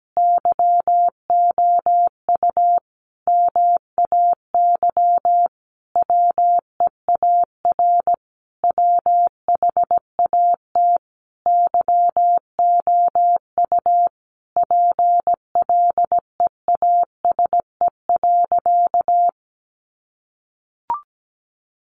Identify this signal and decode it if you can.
Morse code, heard as a keyed tone, about 17 wpm, 700 Hz. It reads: YOU MAY WEAR WHAT YOU PLEASE.
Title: Sentences from Top 1500 Words 17wpm